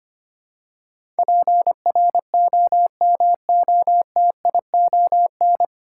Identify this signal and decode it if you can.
Morse code, heard as a keyed tone, about 25 wpm, 700 Hz. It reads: PROMOTION
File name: practice / 25wpm / 9 Letter Words - First 3 Letters Given - 25wpm